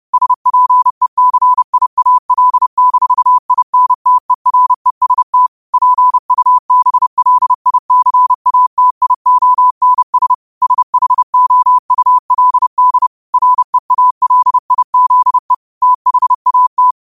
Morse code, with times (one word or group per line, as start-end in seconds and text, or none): 0.1-5.5 SPECIAL-INTEREST
5.7-10.3 PUBLICATIONS
10.6-13.1 SHOULD
13.3-15.5 REALIZE
15.8-16.9 THAT